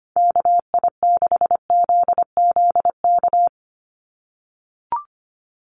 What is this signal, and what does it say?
Morse code, heard as a keyed tone, about 25 wpm, 700 Hz. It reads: KI6ZZK
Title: USA Call Signs 25wpm